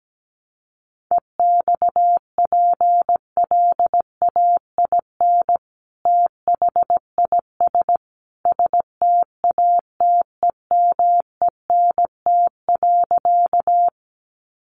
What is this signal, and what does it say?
Morse code, heard as a keyed tone, about 17 wpm, 700 Hz. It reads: EXPLAIN THIS STATEMENT.
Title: Sentences from Top 1200 Words 17wpm